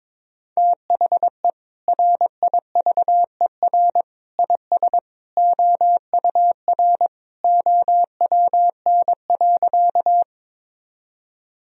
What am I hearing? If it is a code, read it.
Morse code, heard as a keyed tone, about 22 wpm, 700 Hz. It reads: THE RIVER IS OUR OWN.
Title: Sentences from Top 1000 Words - Review Part 1 - 22wpm